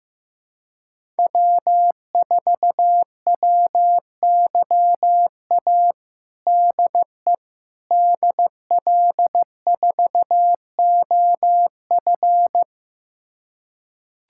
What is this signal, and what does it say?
Morse code, heard as a keyed tone, about 15 wpm, 700 Hz. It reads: W4WYA DE DL4OF
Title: USA Callsign DE Callsign - 5 Characters - 15wpm